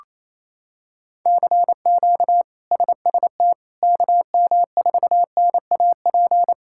Morse code, heard as a keyed tone, about 28 wpm, 700 Hz. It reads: CQ SST KM4NAP
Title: K1USN SST QSO Practice - 28wpm